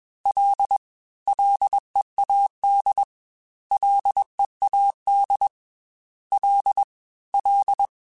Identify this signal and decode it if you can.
Morse code, heard as a keyed tone, about 21 wpm, 800 Hz. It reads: L LEAD LEAD L L
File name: F and L / F and L mp3 / F and L Prefixes 3x2 Twice @20